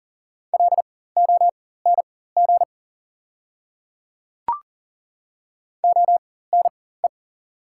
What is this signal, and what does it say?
Morse code, heard as a keyed tone, about 40 wpm, spaced as for 19 wpm, 690 Hz. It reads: LONG ONE
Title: 4x Character Spacing - Top 100 Words - 40wpm